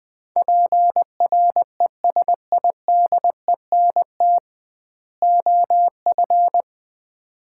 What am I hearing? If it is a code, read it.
Morse code, heard as a keyed tone, about 20 wpm, 700 Hz. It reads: PRESIDENT OF
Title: 2x Word-Spacing - Sets of 3 Words - 20wpm